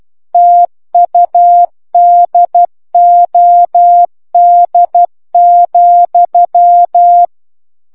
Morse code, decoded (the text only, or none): TUDOD,